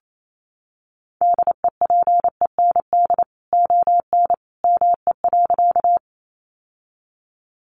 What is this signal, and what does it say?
Morse code, heard as a keyed tone, about 28 wpm, 700 Hz. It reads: DEPEND ON ME.
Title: Sentences from Top 1300 Words 28wpm